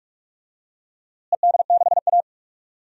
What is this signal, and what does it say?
Morse code, heard as a keyed tone, about 46 wpm, 700 Hz. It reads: ED6A
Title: DXCC-100 Call Signs - 4 Characters - Repeated 3x - 45wpm